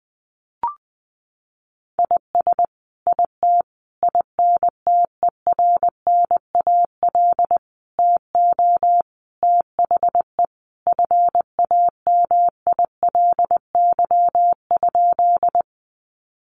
Morse code, heard as a keyed tone, about 20 wpm, 700 Hz. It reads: IS IT INTERNAL TO THE FAMILY?
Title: Sentences from Top 1400 Words 20wpm